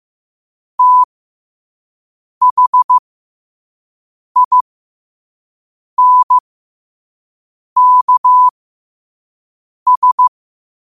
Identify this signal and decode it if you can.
Morse code, heard as a keyed tone, about 15 wpm, spaced as for 5 wpm, 1000 Hz. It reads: THINKS
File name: qotdfiles / 5wpm